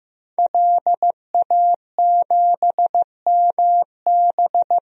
Morse code, heard as a keyed tone, about 15 wpm, 700 Hz. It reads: LA7MB
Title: DXCC-100 Call Signs - 5 Characters - Repeated 2x - 15wpm